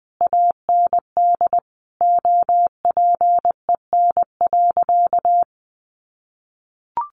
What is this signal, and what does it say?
Morse code, heard as a keyed tone, about 20 wpm, 700 Hz. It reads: AND OPEN.
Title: Sentences from Top 1400 Words 20wpm